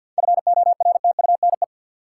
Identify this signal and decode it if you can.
Morse code, heard as a keyed tone, about 51 wpm, 700 Hz. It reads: FORTUNE
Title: Seven Letter Words - 50wpm